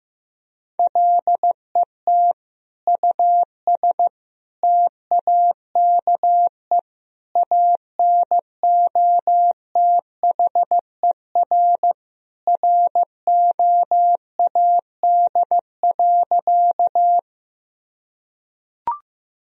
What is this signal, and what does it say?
Morse code, heard as a keyed tone, about 15 wpm, 700 Hz. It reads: LET US TAKE ANOTHER ROAD.
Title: Sentences from Top 1000 Words - Review Part 2 - 15wpm